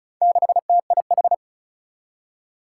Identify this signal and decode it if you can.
Morse code, heard as a keyed tone, about 36 wpm, 700 Hz. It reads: 6TIH
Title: Four Challenging Letter-Number Combinations 35wpm